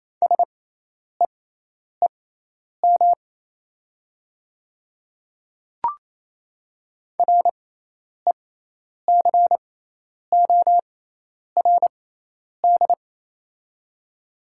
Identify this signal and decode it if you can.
Morse code, heard as a keyed tone, about 28 wpm, spaced as for 10 wpm, 700 Hz. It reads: SEEM RECORD
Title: 6x Character Spacing - Top 500 Words - 28wpm